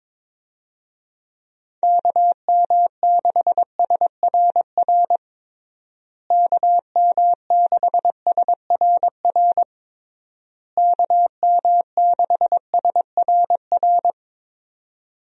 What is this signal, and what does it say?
Morse code, heard as a keyed tone, about 22 wpm, 700 Hz. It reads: KM6SRR KM6SRR KM6SRR
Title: USA Call Signs - 6 Characters - Repeated 3x - 22wpm